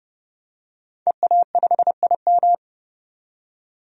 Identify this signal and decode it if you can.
Morse code, heard as a keyed tone, about 30 wpm, 700 Hz. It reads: EA5IM
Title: DXCC-100 Call Signs - 5 Characters - 30wpm